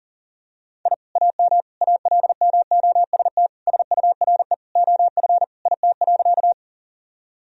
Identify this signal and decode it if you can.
Morse code, heard as a keyed tone, about 37 wpm, 700 Hz. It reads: I AM ALMOST SURE OF IT.
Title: Sentences from Top 1000 Words - Review Part 4 - 40wpm